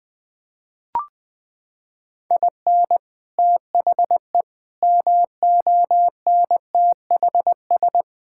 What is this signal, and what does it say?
Morse code, heard as a keyed tone, about 20 wpm, 700 Hz. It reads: IN THE MONTHS
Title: Sentences with 6-Word Phrases - 20wpm